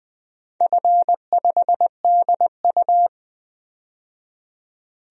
Morse code, heard as a keyed tone, about 20 wpm, 700 Hz. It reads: F5DU
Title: DXCC-100 Call Signs - 4 Characters - 20wpm